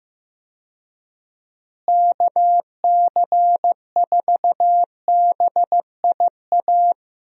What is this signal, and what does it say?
Morse code, heard as a keyed tone, about 16 wpm, 700 Hz. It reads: KC4BIA
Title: USA Call Signs - 6 Characters - Repeated 2x - 15wpm